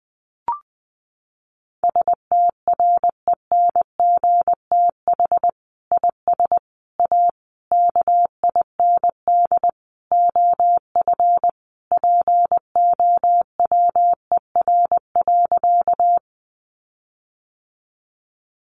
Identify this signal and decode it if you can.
Morse code, heard as a keyed tone, about 20 wpm, 700 Hz. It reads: STRENGTH IS A KIND OF POWER.
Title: Sentences from Top 1300 Words 20wpm